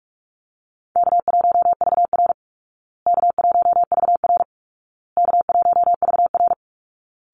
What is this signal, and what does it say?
Morse code, heard as a keyed tone, about 44 wpm, 700 Hz. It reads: K1VR K1VR K1VR
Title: Call Signs Repeated 2x - 45wpm